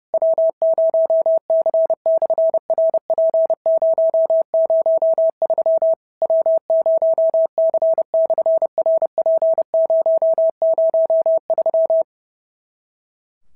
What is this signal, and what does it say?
Morse code, heard as a keyed tone, about 30 wpm, 650 Hz. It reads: W0C/RP003 W0C/RP003
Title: SOTA QSO Practice 30wpm